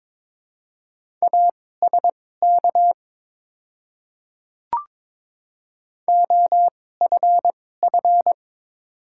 Morse code, heard as a keyed tone, about 22 wpm, spaced as for 16 wpm, 700 Hz. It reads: ASK OFF
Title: 2x Character Spacing - Top 300 Words - 22wpm